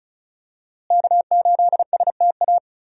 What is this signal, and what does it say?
Morse code, heard as a keyed tone, about 34 wpm, 700 Hz. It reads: K8STA
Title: USA Call Signs - 5 Characters - Repeated 2x - 35wpm